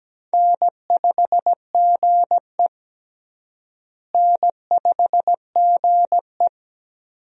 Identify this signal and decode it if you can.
Morse code, heard as a keyed tone, about 17 wpm, 700 Hz. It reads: N5GE N5GE
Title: USA Call Signs - 4 Characters - Repeated 2x - 17wpm